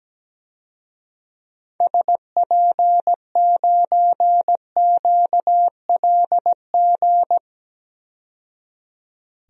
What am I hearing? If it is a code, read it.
Morse code, heard as a keyed tone, about 17 wpm, 700 Hz. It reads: SP9QLG
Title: DXCC-100 Call Signs - 6 Characters - 17wpm